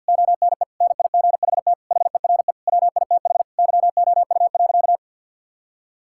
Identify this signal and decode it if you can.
Morse code, heard as a keyed tone, about 50 wpm, 700 Hz. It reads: ONE NIGHT HERE WITH YOU.
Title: Sentences from Top 1600 Words 50wpm